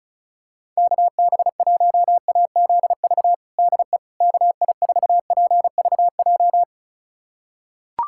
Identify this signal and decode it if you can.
Morse code, heard as a keyed tone, about 36 wpm, 700 Hz. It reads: KB1AZV DE KI4PVJ
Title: USA Callsign DE Callsign - 6 Characters - 35wpm